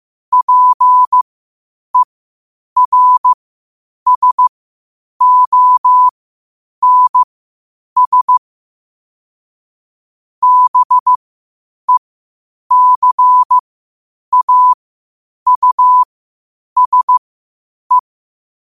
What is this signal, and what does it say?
Morse code, heard as a keyed tone, about 15 wpm, spaced as for 8 wpm, 1000 Hz. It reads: PERSONS BECAUSE
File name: qotdfiles / 8wpm